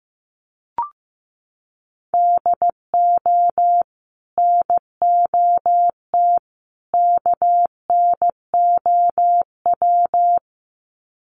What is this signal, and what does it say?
Morse code, heard as a keyed tone, about 15 wpm, 700 Hz. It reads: DO NOT KNOW
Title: Sets of 3 Words 15wpm